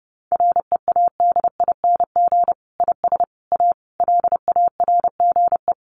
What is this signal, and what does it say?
Morse code, heard as a keyed tone, about 30 wpm, 700 Hz. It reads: READING IS A LARGE
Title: Sentences from Top 1400 Words 30wpm